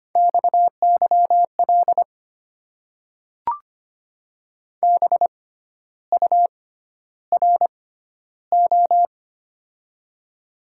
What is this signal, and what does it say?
Morse code, heard as a keyed tone, about 25 wpm, spaced as for 12 wpm, 700 Hz. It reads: XYL BURO
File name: Top 90 QSO Elements - 6x Character Spacing - 25wpm